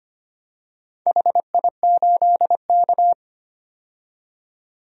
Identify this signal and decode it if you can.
Morse code, heard as a keyed tone, about 25 wpm, 700 Hz. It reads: HI8K
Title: DXCC-100 Call Signs - 4 Characters - 25wpm